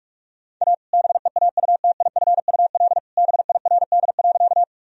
Morse code, heard as a keyed tone, about 42 wpm, 700 Hz. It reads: A BEAUTIFUL BIRD.
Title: Sentences from Top 1000 Words - Review Part 2 - 45wpm